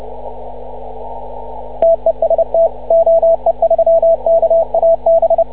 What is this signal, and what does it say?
Morse code, heard as a keyed tone, about 30 wpm, 660 Hz, with radio noise. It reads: TEST OE3KAB